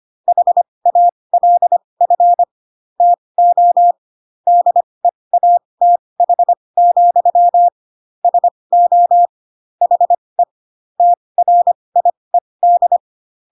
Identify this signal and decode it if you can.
Morse code, heard as a keyed tone, about 25 wpm, spaced as for 20 wpm, 700 Hz. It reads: HALF TO DEATH, SO HE TRIED
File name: Pinocchio / Pinocchio ch1 15 25 ik1hge